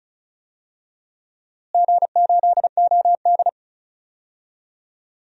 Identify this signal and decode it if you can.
Morse code, heard as a keyed tone, about 34 wpm, 700 Hz. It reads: G8OD
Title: DXCC-100 Call Signs - 4 Characters - Repeated 3x - 35wpm